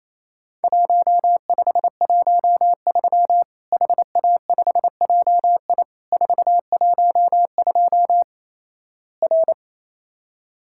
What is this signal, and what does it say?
Morse code, heard as a keyed tone, about 28 wpm, 700 Hz. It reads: 1513 HA5JI 412 R
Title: Worked All Europe DX Contest - QTC Exchange - 28wpm